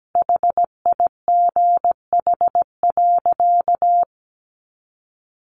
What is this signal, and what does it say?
Morse code, heard as a keyed tone, about 17 wpm, 700 Hz. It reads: HIGH.
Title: Sentences from Top 1400 Words 17wpm